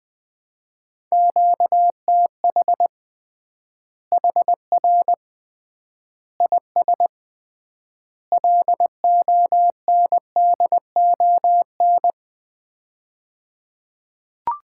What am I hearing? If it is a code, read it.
Morse code, heard as a keyed tone, about 20 wpm, 700 Hz. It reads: QTH HR IS LONDON
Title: Common QSO Phrases - 3x Word-Spacing - 20wpm